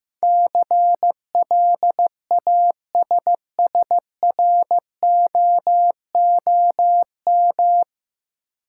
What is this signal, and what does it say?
Morse code, heard as a keyed tone, about 15 wpm, 700 Hz. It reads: CLASSROOM
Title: Sentences from Top 1500 Words - New Word Drill Alt - 15wpm